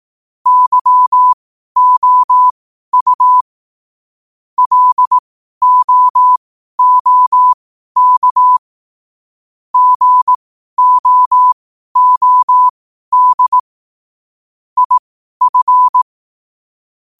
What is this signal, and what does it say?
Morse code, heard as a keyed tone, about 18 wpm, spaced as for 12 wpm, 1000 Hz. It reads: YOU LOOK GOOD IF
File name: qotdfiles / 12wpm